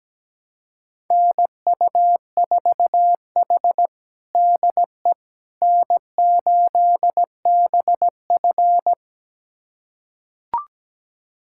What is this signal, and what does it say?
Morse code, heard as a keyed tone, about 17 wpm, 700 Hz. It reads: NU4H DE N8BF
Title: USA Callsign DE Callsign - 4 Characters - 17wpm